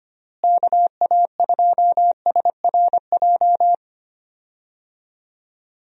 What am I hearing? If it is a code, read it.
Morse code, heard as a keyed tone, about 25 wpm, 700 Hz. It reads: KA2SRJ
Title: USA Call Signs - 6 Characters - Repeated 3x - 25wpm